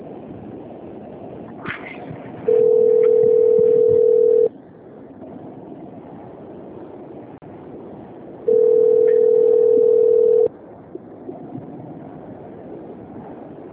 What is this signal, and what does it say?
Ringback tone (North America), 440 + 480 Hz.
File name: rg-1600-unknown- - - .16100